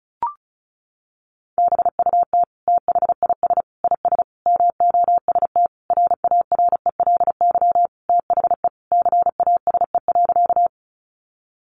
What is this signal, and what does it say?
Morse code, heard as a keyed tone, about 34 wpm, 700 Hz. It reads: BUT THIS IS MOST RARELY THE CASE.
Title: Sentences from Top 1400 Words 35wpm